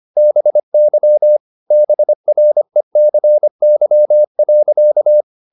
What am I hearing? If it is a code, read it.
Morse code, heard as a keyed tone, about 25 wpm, 600 Hz. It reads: BY BRECY.
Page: Ferie 25 wpm